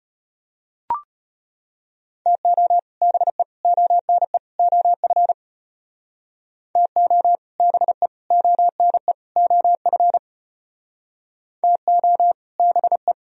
Morse code, heard as a keyed tone, about 34 wpm, 700 Hz. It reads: TO BE ONE OF TO BE ONE OF TO BE